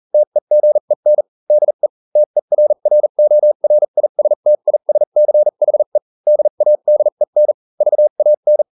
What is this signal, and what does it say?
Morse code, heard as a keyed tone, about 40 wpm, spaced as for 35 wpm, 600 Hz. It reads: TEGEN DE TERRORISTISCHE DADEN VAN